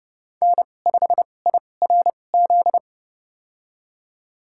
Morse code, heard as a keyed tone, about 30 wpm, spaced as for 22 wpm, 690 Hz. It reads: N5IRZ
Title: USA Call Signs - 5 Characters - 2x Character Spacing - 30wpm